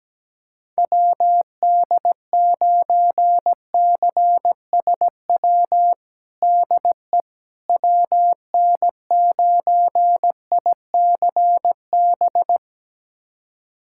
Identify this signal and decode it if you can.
Morse code, heard as a keyed tone, about 17 wpm, 700 Hz. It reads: WD9CSW DE WN9ICB
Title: USA Callsign DE Callsign 17wpm